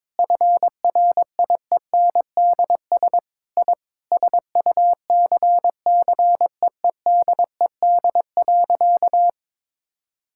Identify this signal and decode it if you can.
Morse code, heard as a keyed tone, about 22 wpm, 700 Hz. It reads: FRIENDS I SUCCEEDED.
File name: Sentences with 7-Word Phrases - 22wpm